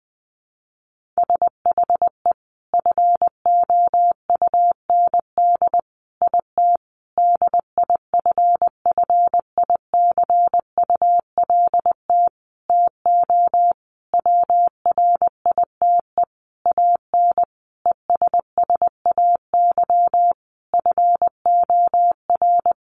Morse code, heard as a keyed tone, about 20 wpm, 700 Hz. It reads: SHE FOUND IT DIFFICULT TO WRITE AN ESSAY FOR
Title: Sentences from Top 1500 Words 20wpm